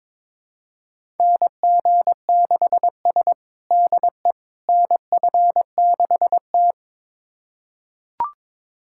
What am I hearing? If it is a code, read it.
Morse code, heard as a keyed tone, about 22 wpm, 700 Hz. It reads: NG6S DE NF6T
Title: USA Callsign DE Callsign - 4 Characters - 22wpm